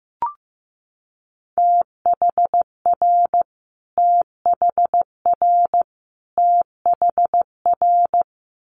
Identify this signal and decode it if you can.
Morse code, heard as a keyed tone, about 15 wpm, 700 Hz. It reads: THR THR THR
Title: Multi-letter Phonemes Repeated 3x 15wpm